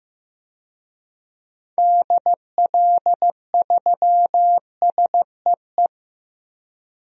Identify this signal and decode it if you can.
Morse code, heard as a keyed tone, about 15 wpm, 700 Hz. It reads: DL3SEE
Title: DXCC-100 Call Signs - 6 Characters - 15wpm